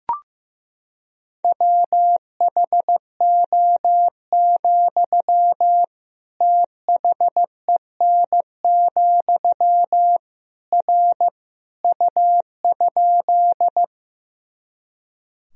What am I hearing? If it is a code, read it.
Morse code, heard as a keyed tone, about 15 wpm, 700 Hz. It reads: WHO, THEN, R U?
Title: Sentences from Top 100 Words and Abbreviations 15wpm